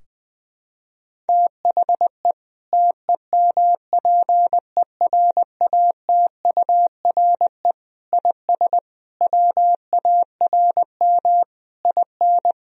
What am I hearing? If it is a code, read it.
Morse code, heard as a keyed tone, about 20 wpm, 700 Hz. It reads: THE TEMPERATURE IS WARM IN
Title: Sentences from Top 1000 Words - Review Part 4 - 20wpm